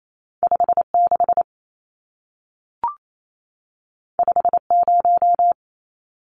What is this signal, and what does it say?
Morse code, heard as a keyed tone, about 28 wpm, 700 Hz. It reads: 56 50
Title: Numbers - 2-Digit 28wpm